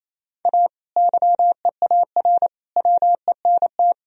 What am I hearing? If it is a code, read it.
Morse code, heard as a keyed tone, about 28 wpm, 700 Hz. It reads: A YEAR WENT
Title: Sentences from Top 1000 Words - Review Part 1 - 28wpm